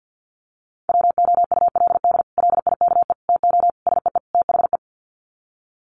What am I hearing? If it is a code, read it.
Morse code, heard as a keyed tone, about 50 wpm, 700 Hz. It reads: WOULD LIKE TO SEE THE
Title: Sets of 5 Words Encore2 50wpm